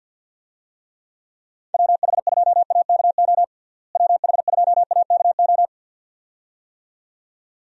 Morse code, heard as a keyed tone, about 47 wpm, 700 Hz. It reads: WH2AKO WH2AKO
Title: USA Call Signs - 6 Characters - Repeated 2x - 50wpm